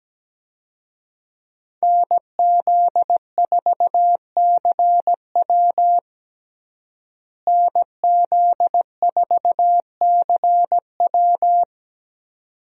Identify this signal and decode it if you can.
Morse code, heard as a keyed tone, about 17 wpm, 700 Hz. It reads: NZ4CW NZ4CW
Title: USA Call Signs - 5 Characters - Repeated 2x - 17wpm